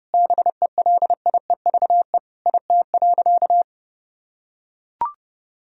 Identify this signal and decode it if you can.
Morse code, heard as a keyed tone, about 32 wpm, 700 Hz. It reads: BELIEVE IT.
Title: Sentences from Top 1000 Words - Review Part 1 - 30wpm